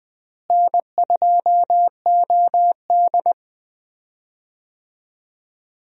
Morse code, heard as a keyed tone, about 20 wpm, 700 Hz. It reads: N2OD
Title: USA Call Signs - 4 Characters - Repeated 3x - 20wpm